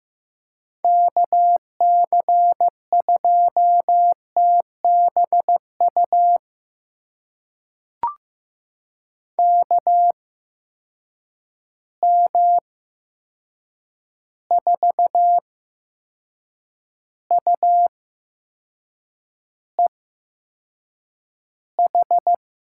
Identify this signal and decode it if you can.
Morse code, heard as a keyed tone, about 15 wpm, spaced as for 6 wpm, 700 Hz. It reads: KC2TBU KM4UEH
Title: USA Call Signs - 6 Characters - 8x Character Spacing - 15wpm